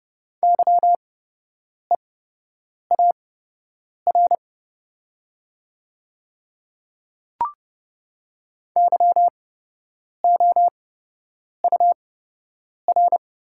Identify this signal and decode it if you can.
Morse code, heard as a keyed tone, about 30 wpm, spaced as for 8 wpm, 690 Hz. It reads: YEAR YOUR
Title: 8x Character Spacing - Top 200 Words - 30wpm